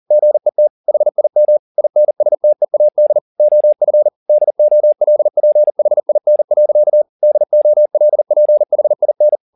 Morse code, heard as a keyed tone, about 40 wpm, 600 Hz. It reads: GET HIM INSTEAD OF DOLPHIN. DOLPHIN